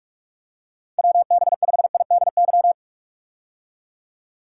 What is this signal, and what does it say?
Morse code, heard as a keyed tone, about 41 wpm, 700 Hz. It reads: WB5IDY
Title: K1USN SST QSO Practice - 45wpm